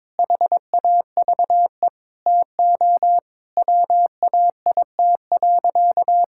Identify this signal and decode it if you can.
Morse code, heard as a keyed tone, about 22 wpm, 700 Hz. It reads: HAVE TO WAIT.
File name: practice / 22wpm / Sentences with 3-Word Phrases - 22wpm